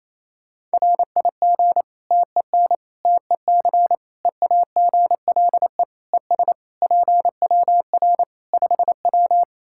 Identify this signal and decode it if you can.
Morse code, heard as a keyed tone, about 27 wpm, 700 Hz. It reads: RIG TEN TEC EAGLE ES PWR 5W
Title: Practice QSO - Exchange 2 - Info - 28wpm